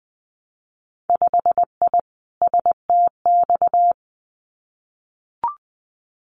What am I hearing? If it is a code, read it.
Morse code, heard as a keyed tone, about 20 wpm, 700 Hz. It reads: 5I STX
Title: Winter Field Day Exchanges Encore 20wpm